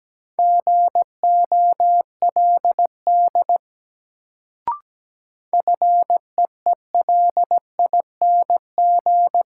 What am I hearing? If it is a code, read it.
Morse code, heard as a keyed tone, about 17 wpm, 700 Hz. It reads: GOLD FEELING
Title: Sentences from Top 1000 Words - Review Part 3 - New Word Drill - 17wpm